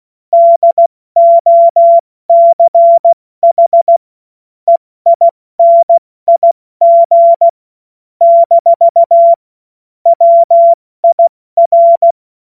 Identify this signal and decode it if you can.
Morse code, heard as a keyed tone, about 16 wpm, 680 Hz. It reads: DOCH EINIG - WIR
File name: Morsekurs / Witze / Witz 17. Januar 2023 70 F80